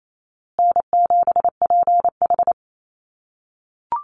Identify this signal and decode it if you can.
Morse code, heard as a keyed tone, about 28 wpm, 700 Hz. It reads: N7PH
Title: USA Call Signs Repeated 3x 28wpm